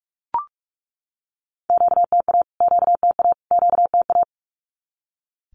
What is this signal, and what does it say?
Morse code, heard as a keyed tone, about 44 wpm, 700 Hz. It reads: QTA QTA QTA
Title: Q-Codes Repeated 3x 45wpm